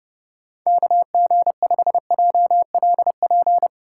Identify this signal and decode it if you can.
Morse code, heard as a keyed tone, about 32 wpm, 700 Hz. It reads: KG5JLP
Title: USA Call Signs - 6 Characters - 2x Character Spacing - 30wpm